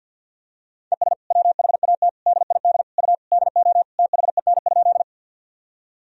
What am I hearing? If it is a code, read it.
Morse code, heard as a keyed tone, about 52 wpm, 700 Hz. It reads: ES WHAT DID U DO THEN?
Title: Sentences from Top 100 Words and Abbreviations 50wpm